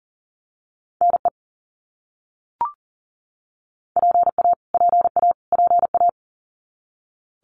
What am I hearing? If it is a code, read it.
Morse code, heard as a keyed tone, about 40 wpm, 700 Hz. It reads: NE PA PA PA
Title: US States Repeated 3x 40wpm